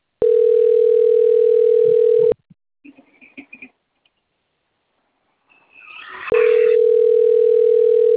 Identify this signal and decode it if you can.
Ringback tone (North America), 440 + 480 Hz.